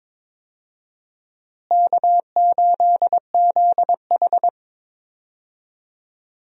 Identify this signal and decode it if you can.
Morse code, heard as a keyed tone, about 22 wpm, 700 Hz. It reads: K8ZH
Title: USA Call Signs - 4 Characters - Repeated 3x - 22wpm